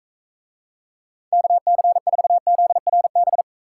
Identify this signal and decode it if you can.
Morse code, heard as a keyed tone, about 44 wpm, 700 Hz. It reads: KC4ZRB